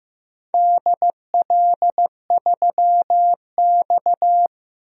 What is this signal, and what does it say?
Morse code, heard as a keyed tone, about 15 wpm, 700 Hz. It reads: DL3X
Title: DXCC-100 Call Signs - 4 Characters - 15wpm